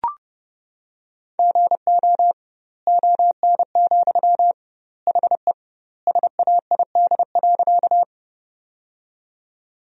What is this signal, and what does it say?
Morse code, heard as a keyed tone, about 30 wpm, 700 Hz. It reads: GO ON, HE SAID.
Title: 2x Word-Spacing - Sentences from Top 100 Words - 30wpm